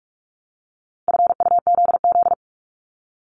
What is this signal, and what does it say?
Morse code, heard as a keyed tone, about 46 wpm, 700 Hz. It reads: FUZZ
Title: Multi-letter Phonemes 45wpm